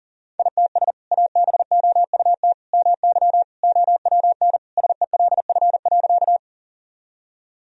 Morse code, heard as a keyed tone, about 40 wpm, 700 Hz. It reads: ITS ABOUT MY OWN SELF.
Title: Sentences from Top 1000 Words - Review Part 1 - 40wpm